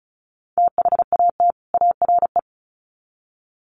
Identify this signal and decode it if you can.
Morse code, heard as a keyed tone, about 34 wpm, 700 Hz. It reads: THAT ARE
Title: Sets of 2 Words 35wpm